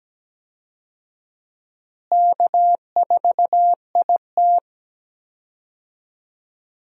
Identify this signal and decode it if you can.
Morse code, heard as a keyed tone, about 17 wpm, 700 Hz. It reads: K4IT
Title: USA Call Signs - 4 Characters - Repeated 3x - 17wpm